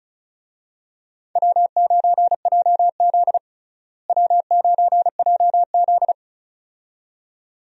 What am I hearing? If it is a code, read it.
Morse code, heard as a keyed tone, about 36 wpm, 700 Hz. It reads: W9JZ W9JZ
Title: USA Call Signs - 4 Characters - Repeated 2x - 35wpm